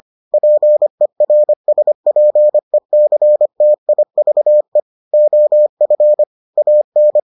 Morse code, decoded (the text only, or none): PERSPECTIVE OF AN